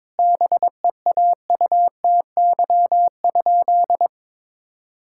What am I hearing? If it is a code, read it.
Morse code, heard as a keyed tone, about 22 wpm, 700 Hz. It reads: BEAUTY?
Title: Sentences from Top 1000 Words - Review Part 5 - 22wpm